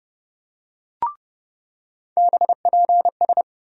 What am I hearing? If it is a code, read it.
Morse code, heard as a keyed tone, about 30 wpm, 700 Hz. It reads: BPS
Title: USA Call Sign Postfixes - 3 Letters 30wpm